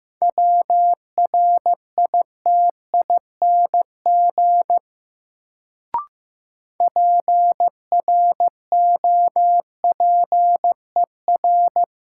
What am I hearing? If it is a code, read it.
Morse code, heard as a keyed tone, about 15 wpm, 700 Hz. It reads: WRITING PROPER
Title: Sentences from Top 800 Words - New Word Drill - 15wpm